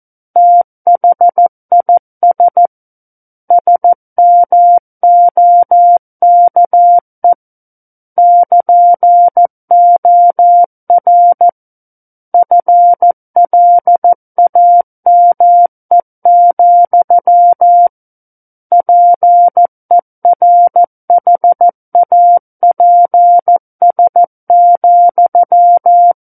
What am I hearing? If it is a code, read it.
Morse code, heard as a keyed tone, about 14 wpm, 700 Hz. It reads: THIS SMOKE <KN>OR FLAME, PERHAPS,
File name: Book One / 05-Chapter 5 - 14 WPM